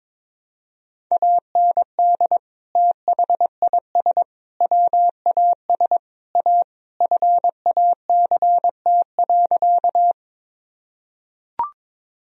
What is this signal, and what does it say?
Morse code, heard as a keyed tone, about 22 wpm, 700 Hz. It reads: AND THIS WAS A FACT.
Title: Sentences from Top 1000 Words - Review Part 3 - 22wpm